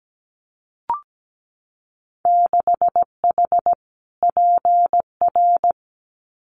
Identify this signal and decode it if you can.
Morse code, heard as a keyed tone, about 17 wpm, 700 Hz. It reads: 6H PR
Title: Winter Field Day Exchanges Encore 17wpm